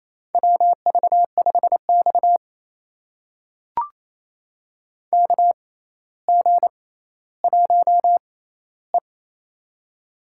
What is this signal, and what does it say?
Morse code, heard as a keyed tone, about 28 wpm, spaced as for 14 wpm, 700 Hz. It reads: WV5X KG1E
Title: USA Call Signs - 4 Characters - 6x Character Spacing - 28wpm